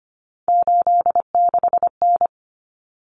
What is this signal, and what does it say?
Morse code, heard as a keyed tone, about 25 wpm, 700 Hz. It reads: 86N
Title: Common Cut Numbers - 3-Digit 25wpm